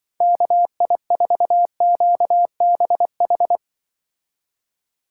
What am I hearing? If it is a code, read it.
Morse code, heard as a keyed tone, about 24 wpm, 700 Hz. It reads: KI4QBH
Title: Speed Racing - USA 2x3 Call Signs 20wpm